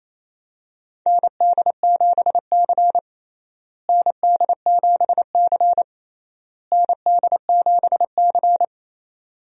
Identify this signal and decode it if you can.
Morse code, heard as a keyed tone, about 28 wpm, 700 Hz. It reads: ND7C ND7C ND7C
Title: USA Call Signs - 4 Characters - Repeated 3x - 28wpm